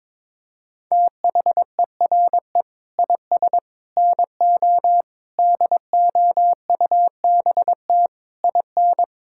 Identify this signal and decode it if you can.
Morse code, heard as a keyed tone, about 22 wpm, 700 Hz. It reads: THERE IS NO DOUBT IN
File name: Sets of 9 Words - 22wpm